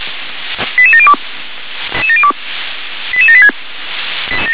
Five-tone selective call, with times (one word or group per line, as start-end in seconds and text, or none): 0.8-1.1 88702
3.1-3.5 88786